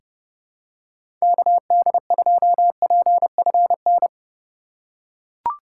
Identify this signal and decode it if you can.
Morse code, heard as a keyed tone, about 32 wpm, 700 Hz. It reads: KD2PFN
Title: USA Call Signs - 6 Characters - 2x Character Spacing - 30wpm